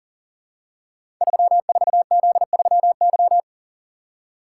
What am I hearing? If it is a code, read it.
Morse code, heard as a keyed tone, about 40 wpm, 700 Hz. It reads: 34Z3Y
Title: Code Groups - 5 Letter:Numbers - 40wpm